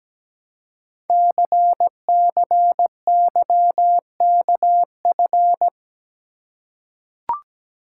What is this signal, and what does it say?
Morse code, heard as a keyed tone, about 17 wpm, 700 Hz. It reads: CCYKF
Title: Code Groups - 5 Letters - 17wpm